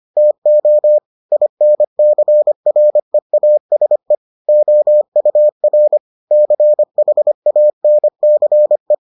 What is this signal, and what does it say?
Morse code, heard as a keyed tone, about 25 wpm, 600 Hz. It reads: TO INCREASE OUR CHANCE